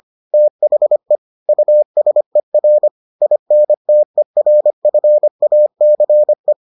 Morse code, heard as a keyed tone, about 25 wpm, 600 Hz. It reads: THE USER INTERFACE